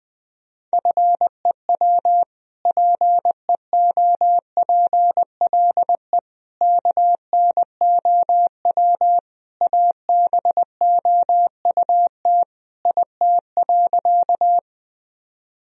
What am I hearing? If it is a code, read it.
Morse code, heard as a keyed tone, about 20 wpm, 700 Hz. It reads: FEW PEOPLE KNOW ABOUT IT.
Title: Sentences from Top 1000 Words - Review Part 1 - 20wpm